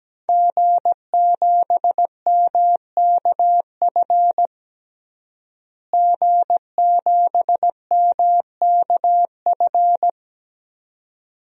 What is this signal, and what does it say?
Morse code, heard as a keyed tone, about 17 wpm, 700 Hz. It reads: G7MKF G7MKF
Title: DXCC-100 Call Signs - 5 Characters - Repeated 2x - 17wpm